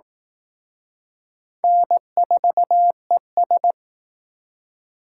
Morse code, heard as a keyed tone, about 18 wpm, 700 Hz. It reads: N4ES
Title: Speed Racing - USA Call Signs 15wpm